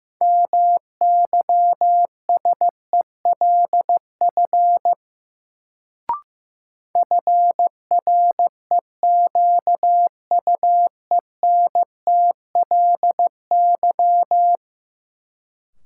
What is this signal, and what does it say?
Morse code, heard as a keyed tone, about 15 wpm, 700 Hz. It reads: MYSELF FREQUENTLY
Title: Sentences from Top 1300 Words - New Word Drill - 15wpm